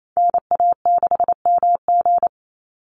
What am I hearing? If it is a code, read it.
Morse code, heard as a keyed tone, about 28 wpm, 700 Hz. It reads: NA6MG
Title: NA SOTA Activator Callsigns 28wpm